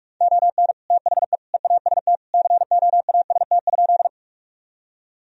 Morse code, heard as a keyed tone, about 44 wpm, 700 Hz. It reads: ON THE EAST COAST?
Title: Sentences from Top 1000 Words - Review Part 4 - 45wpm